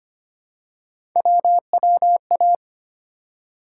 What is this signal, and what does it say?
Morse code, heard as a keyed tone, about 25 wpm, 700 Hz. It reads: WWA
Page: morse-code-ninja-practice-25wpm